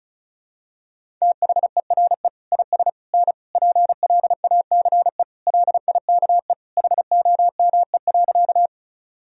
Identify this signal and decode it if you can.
Morse code, heard as a keyed tone, about 34 wpm, 700 Hz. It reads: THERE IS N PLACE LIKE HOME.
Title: Sentences from Top 200 Words and Abbreviations 35wpm